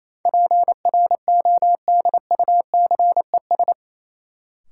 Morse code, heard as a keyed tone, about 28 wpm, 700 Hz. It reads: PRODUCES
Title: Sentences from Top 1000 Words - Review Part 6 - New Word Drill - 28wpm